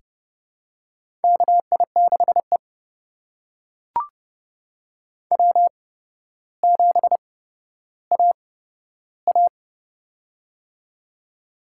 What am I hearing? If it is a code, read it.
Morse code, heard as a keyed tone, about 29 wpm, spaced as for 13 wpm, 700 Hz. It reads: KI6E W7AA
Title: USA Call Signs - 8x Character Spacing - 30wpm